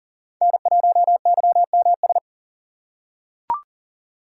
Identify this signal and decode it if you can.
Morse code, heard as a keyed tone, about 40 wpm, 700 Hz. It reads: N1YMS
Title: USA Call Signs - 5 Characters - 6x Character Spacing - 40wpm